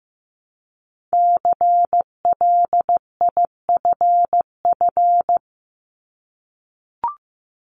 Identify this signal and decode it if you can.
Morse code, heard as a keyed tone, about 15 wpm, 700 Hz. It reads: CLIFF
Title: Multi-letter Phonemes Repeated 3x 15wpm